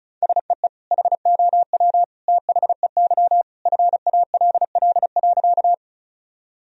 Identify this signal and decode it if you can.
Morse code, heard as a keyed tone, about 36 wpm, 700 Hz. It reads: SEE HOW THEY FALL.
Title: Sentences from Top 1000 Words - Review Part 2 - 35wpm